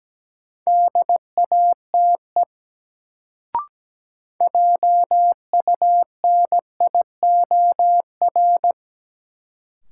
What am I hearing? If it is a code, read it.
Morse code, heard as a keyed tone, about 17 wpm, 700 Hz. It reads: DATE JUNIOR
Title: Sentences from Top 1000 Words - Review Part 7 - New Word Drill - 17wpm